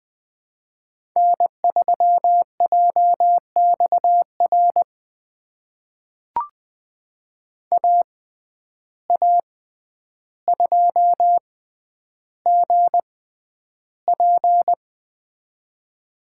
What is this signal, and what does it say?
Morse code, heard as a keyed tone, about 20 wpm, spaced as for 10 wpm, 700 Hz. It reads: N3JXR AA2GP
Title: USA Call Signs - 5 Characters - 6x Character Spacing - 20wpm